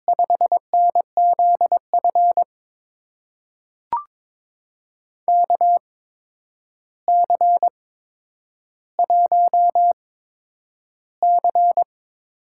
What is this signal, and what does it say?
Morse code, heard as a keyed tone, about 22 wpm, spaced as for 9 wpm, 700 Hz. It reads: 5NZF KC1C
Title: USA Call Signs - 6 Characters - 8x Character Spacing - 22wpm